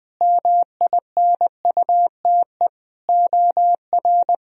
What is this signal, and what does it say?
Morse code, heard as a keyed tone, about 20 wpm, 700 Hz. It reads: MINUTE OR
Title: Sentences from Top 1000 Words - Review Part 1 - 20wpm